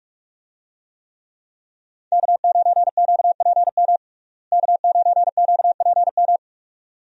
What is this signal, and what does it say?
Morse code, heard as a keyed tone, about 46 wpm, 700 Hz. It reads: K9QPM K9QPM
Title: USA Call Signs - 5 Characters - Repeated 2x - 45wpm